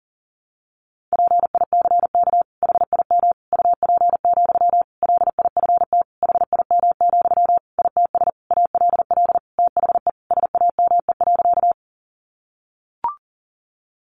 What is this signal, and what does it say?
Morse code, heard as a keyed tone, about 40 wpm, 700 Hz. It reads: PICK HIM UP, LIFT HIM, ITS ALL THE SAME.
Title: Sentences from Top 1000 Words 40wpm